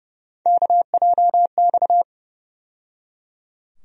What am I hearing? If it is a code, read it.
Morse code, heard as a keyed tone, about 30 wpm, 700 Hz. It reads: KJX
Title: USA Call Sign Postfixes - 3 Letters 30wpm